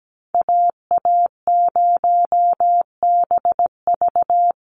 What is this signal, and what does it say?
Morse code, heard as a keyed tone, about 17 wpm, 700 Hz. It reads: AA0BV
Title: NA SOTA Activator Callsigns 17wpm